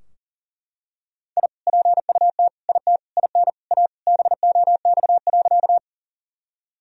Morse code, heard as a keyed tone, about 40 wpm, 700 Hz. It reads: I PUT IT IN A BOX.
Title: Sentences from Top 1000 Words - Review Part 2 - 40wpm